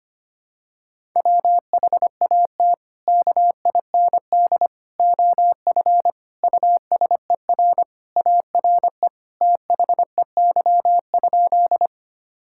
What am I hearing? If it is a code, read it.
Morse code, heard as a keyed tone, about 25 wpm, 700 Hz. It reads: WHAT KIND OF USER ARE THEY?
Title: Sentences from Top 1000 Words - Review Part 5 - 25wpm